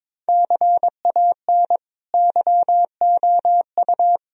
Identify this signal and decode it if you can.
Morse code, heard as a keyed tone, about 22 wpm, 700 Hz. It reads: CAN YOU
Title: Sentences from Top 1000 Words - Review Part 2 - 22wpm